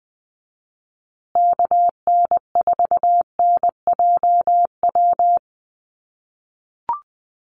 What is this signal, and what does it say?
Morse code, heard as a keyed tone, about 20 wpm, 700 Hz. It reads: KN4NJW
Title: USA Call Signs Repeated 2x 20wpm